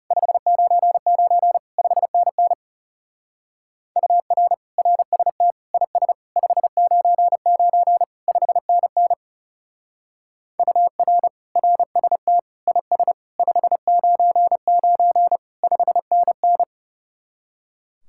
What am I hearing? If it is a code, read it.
Morse code, heard as a keyed tone, about 34 wpm, 700 Hz. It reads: 599 5NN UR RST IS 599 5NN UR RST IS 599 5NN
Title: Speed-Racing Common QSO Phrases - 30wpm